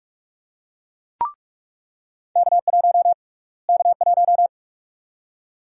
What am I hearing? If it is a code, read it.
Morse code, heard as a keyed tone, about 46 wpm, 700 Hz. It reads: K1 K1
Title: USA Call Sign Prefixes - Repeated 2x - 45wpm